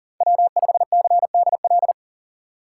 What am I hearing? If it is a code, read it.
Morse code, heard as a keyed tone, about 40 wpm, 700 Hz. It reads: W5CDL
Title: USA Call Signs - 5 Characters - 3x Character Spacing - 40wpm